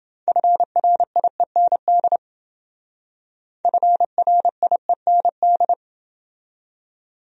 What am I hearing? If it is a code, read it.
Morse code, heard as a keyed tone, about 28 wpm, 700 Hz. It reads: FRIEND FRIEND